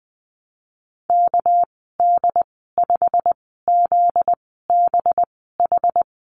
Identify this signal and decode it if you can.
Morse code, heard as a keyed tone, about 19 wpm, spaced as for 14 wpm, 700 Hz. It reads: KD5ZBH
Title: 245 Lesson 39g - Letter z - Call Signs 20wpm